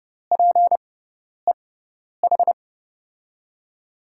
Morse code, heard as a keyed tone, about 30 wpm, spaced as for 10 wpm, 700 Hz. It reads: PEH
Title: USA Call Signs - 5 Characters - 6x Character Spacing - 30wpm